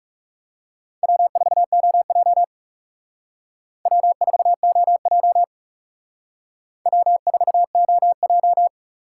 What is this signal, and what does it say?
Morse code, heard as a keyed tone, about 37 wpm, 700 Hz. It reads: W4OJ W4OJ W4OJ
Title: Speed Racing - USA 1x2 Call Signs 30wpm